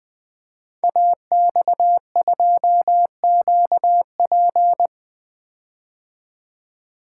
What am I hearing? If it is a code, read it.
Morse code, heard as a keyed tone, about 20 wpm, 700 Hz. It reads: AX2QP
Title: Code Groups - 5 Letter:Numbers - 20wpm